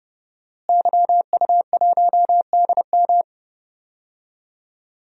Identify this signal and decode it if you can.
Morse code, heard as a keyed tone, about 30 wpm, 700 Hz. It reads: YU1DM
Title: DXCC-100 Call Signs - 5 Characters - Repeated 3x - 30wpm